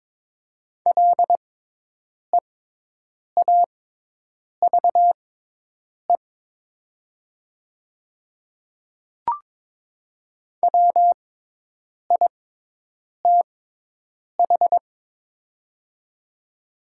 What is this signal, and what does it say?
Morse code, heard as a keyed tone, about 22 wpm, spaced as for 8 wpm, 700 Hz. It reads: LEAVE WITH
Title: 6x Character Spacing - Top 500 Words - 22wpm